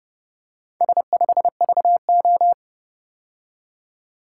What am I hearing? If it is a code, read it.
Morse code, heard as a keyed tone, about 28 wpm, 700 Hz. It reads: S5VO